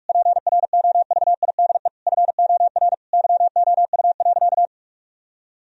Morse code, heard as a keyed tone, about 46 wpm, 700 Hz. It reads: PROVIDE FOR YOU.
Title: Sentences from Top 1000 Words - Review Part 3 - 45wpm